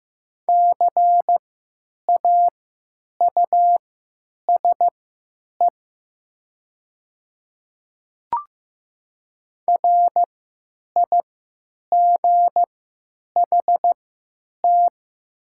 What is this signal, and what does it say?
Morse code, heard as a keyed tone, about 15 wpm, spaced as for 9 wpm, 700 Hz. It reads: CAUSE RIGHT
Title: 3x Character Spacing - Top 200 Words - 15wpm